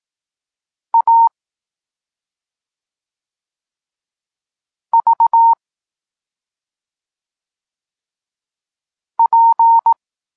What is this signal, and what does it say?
Morse code, heard as a keyed tone, about 18 wpm, 920 Hz. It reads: A V P
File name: Tekst-34 4